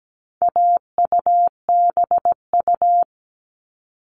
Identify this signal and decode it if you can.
Morse code, heard as a keyed tone, about 17 wpm, 700 Hz. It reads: AUBU
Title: All Cut Numbers - 4-Digit 17wpm